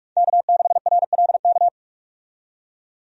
Morse code, heard as a keyed tone, about 47 wpm, 700 Hz. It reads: K6RLK
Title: USA Call Signs - 5 Characters - 45wpm